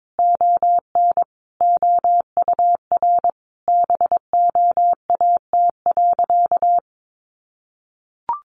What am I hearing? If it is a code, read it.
Morse code, heard as a keyed tone, about 22 wpm, 700 Hz. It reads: ON OUR BOAT.
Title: Sentences from Top 1400 Words 22wpm